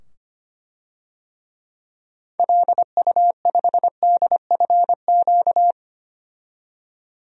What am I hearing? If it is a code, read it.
Morse code, heard as a keyed tone, about 25 wpm, 700 Hz. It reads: LU5DFQ
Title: DXCC-100 Call Signs - 6 Characters - 25wpm